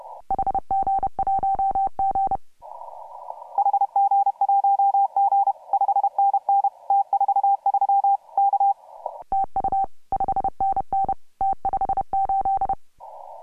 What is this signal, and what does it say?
Morse code, heard as a keyed tone, about 30 wpm, 750 Hz. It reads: HG1G HG1G 5NN T43 K TU 5NN T58
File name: AUDIO2018 / HG1G CW